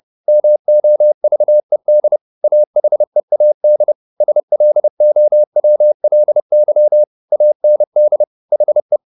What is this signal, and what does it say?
Morse code, heard as a keyed tone, about 30 wpm, 600 Hz. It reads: MOVED AHEAD SLOWLY AND HE